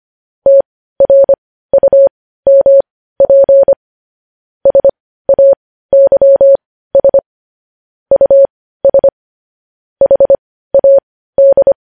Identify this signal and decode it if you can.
Morse code, heard as a keyed tone, about 25 wpm, spaced as for 15 wpm, 550 Hz. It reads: TRUMP SAYS US HAD
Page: news-headlines-in-morse-code-at-15-wpm